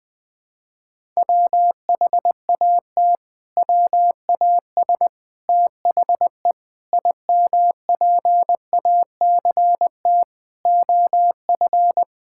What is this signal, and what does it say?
Morse code, heard as a keyed tone, about 21 wpm, 700 Hz. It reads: WHAT WAS THE IMPACT OF
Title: Sentences from Top 1000 Words - Review Part 6 - 20wpm